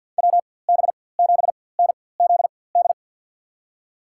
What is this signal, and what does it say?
Morse code, heard as a keyed tone, about 54 wpm, spaced as for 24 wpm, 700 Hz. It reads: WB7NZD
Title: USA Call Signs - 6 Characters - 4x Character Spacing - 50wpm